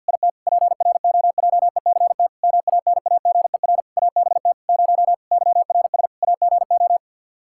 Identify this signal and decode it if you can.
Morse code, heard as a keyed tone, about 52 wpm, 700 Hz. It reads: IT PROJECT MANAGER ABT 0 YRS AGO
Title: Practice QSO - Exchange 3 - More Info - 50wpm